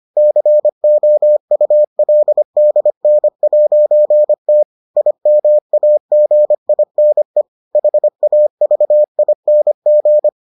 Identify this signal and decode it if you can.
Morse code, heard as a keyed tone, about 25 wpm, 600 Hz. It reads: COULDN'T IMAGINE HAVING